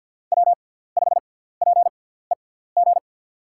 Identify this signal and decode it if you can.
Morse code, heard as a keyed tone, about 52 wpm, spaced as for 17 wpm, 700 Hz. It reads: W5PEG
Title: USA Call Signs - 6x Character Spacing - 50wpm